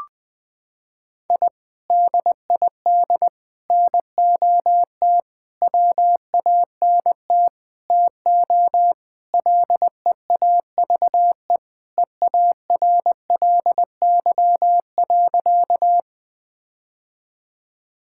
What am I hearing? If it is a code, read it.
Morse code, heard as a keyed tone, about 20 wpm, 700 Hz. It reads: I DID NOT WANT TO LEAVE EARLY.
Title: Sentences with 5-Word Phrases - 20wpm